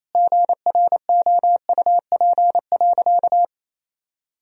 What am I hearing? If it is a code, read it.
Morse code, heard as a keyed tone, about 28 wpm, 700 Hz. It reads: GROUP.
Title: Sentences from Top 1000 Words - Review Part 1 - 28wpm